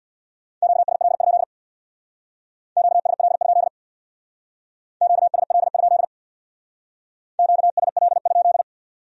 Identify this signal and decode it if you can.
Morse code, heard as a keyed tone, about 63 wpm, 700 Hz. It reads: QSL? QSL? QSL? QSL?